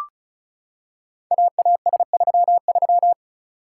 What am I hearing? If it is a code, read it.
Morse code, heard as a keyed tone, about 36 wpm, 700 Hz. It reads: AAS33
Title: Code Groups - 5 Letter:Numbers - 35wpm